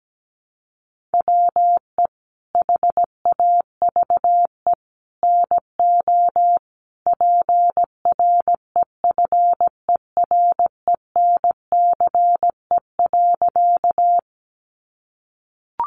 Morse code, heard as a keyed tone, about 17 wpm, 700 Hz. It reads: WE HAVE NO PREFERENCE.
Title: Sentences from Top 1500 Words 17wpm